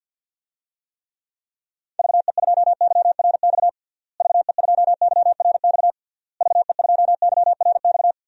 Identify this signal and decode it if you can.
Morse code, heard as a keyed tone, about 51 wpm, 700 Hz. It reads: VE2YRX VE2YRX VE2YRX
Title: DXCC-100 Call Signs - 6 Characters - Repeated 3x - 50wpm